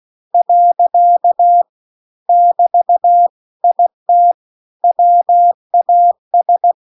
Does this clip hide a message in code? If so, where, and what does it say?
Yes: Morse code from 0.3 to 6.7 s, heard as a keyed tone, about 16 wpm, 700 Hz. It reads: . <BT> IT WAS